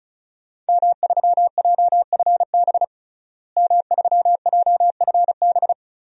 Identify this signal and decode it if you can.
Morse code, heard as a keyed tone, about 34 wpm, 700 Hz. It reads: M3JFB M3JFB
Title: DXCC-100 Call Signs - 5 Characters - Repeated 2x - 35wpm